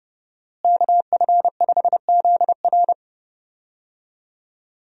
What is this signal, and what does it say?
Morse code, heard as a keyed tone, about 31 wpm, 700 Hz. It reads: KF5ZR
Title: USA Call Signs - 5 Characters - Repeated 2x - 30wpm